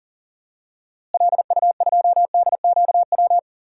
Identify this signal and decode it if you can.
Morse code, heard as a keyed tone, about 38 wpm, 700 Hz. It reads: LU2DQW
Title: DXCC-100 Call Signs - 6 Characters - Repeated 3x - 40wpm